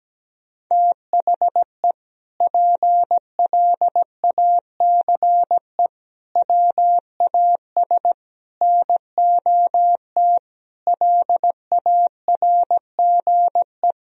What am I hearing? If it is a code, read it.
Morse code, heard as a keyed tone, about 17 wpm, 700 Hz. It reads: THE PLACE WAS NOT LARGE
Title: Sentences from Top 1000 Words - Review Part 1 - 17wpm